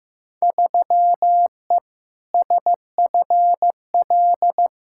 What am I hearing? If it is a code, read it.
Morse code, heard as a keyed tone, about 15 wpm, 700 Hz. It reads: 3E SFL
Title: Field Day Exchanges Encore - 15wpm